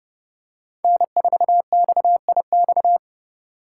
Morse code, heard as a keyed tone, about 29 wpm, 700 Hz. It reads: N4XIX
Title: USA Call Signs - 5 Characters - 2x Character Spacing - 30wpm